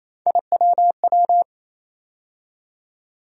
Morse code, heard as a keyed tone, about 27 wpm, 700 Hz. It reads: IWW
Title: USA Call Sign Postfixes - 3 Letters 28wpm